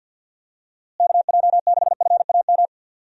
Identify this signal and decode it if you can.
Morse code, heard as a keyed tone, about 51 wpm, 700 Hz. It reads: KJ6FAM
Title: USA Call Signs - 6 Characters - Repeated 2x - 50wpm